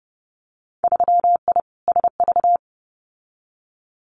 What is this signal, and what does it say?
Morse code, heard as a keyed tone, about 30 wpm, 700 Hz. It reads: 3I SV
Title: Winter Field Day Exchanges Encore 30wpm